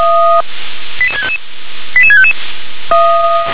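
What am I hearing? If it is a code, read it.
Five-tone selective call (ZVEI), heard as quick runs of five tones, tones of about 70 ms. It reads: 88655 88655
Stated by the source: Signal received via SDR, ZVEI five-tone.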